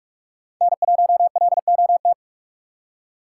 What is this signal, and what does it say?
Morse code, heard as a keyed tone, about 45 wpm, 700 Hz. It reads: N1LOT